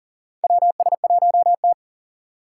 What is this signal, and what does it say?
Morse code, heard as a keyed tone, about 40 wpm, 700 Hz. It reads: WS1T
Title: USA Call Signs - 2x Character Spacing - 40wpm